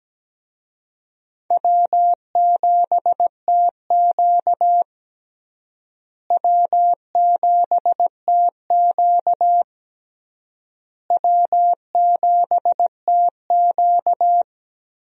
Morse code, heard as a keyed tone, about 17 wpm, 700 Hz. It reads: W7TQ W7TQ W7TQ
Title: USA Call Signs - 4 Characters - Repeated 3x - 17wpm